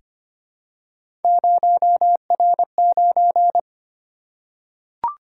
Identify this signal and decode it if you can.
Morse code, heard as a keyed tone, about 25 wpm, 700 Hz. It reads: 0R9
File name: Three Letter-Number Combinations - 25wpm